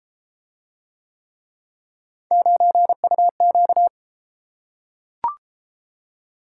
Morse code, heard as a keyed tone, about 32 wpm, 700 Hz. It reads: 9UQ